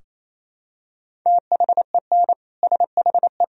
Morse code, heard as a keyed tone, about 28 wpm, 700 Hz. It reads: THEN SHE
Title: Sentences from Top 1000 Words - Review Part 2 - 28wpm